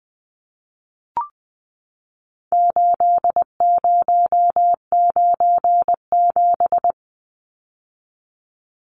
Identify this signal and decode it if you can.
Morse code, heard as a keyed tone, about 20 wpm, 700 Hz. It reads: 8097
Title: Numbers - 4-Digit 20wpm